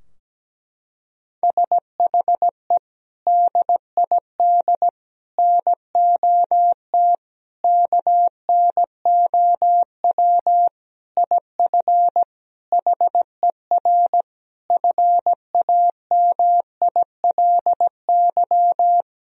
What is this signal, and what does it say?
Morse code, heard as a keyed tone, about 17 wpm, 700 Hz. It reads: SHE DID NOT KNOW IF HER FAMILY
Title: Sentences from Top 1000 Words - Review Part 4 - 17wpm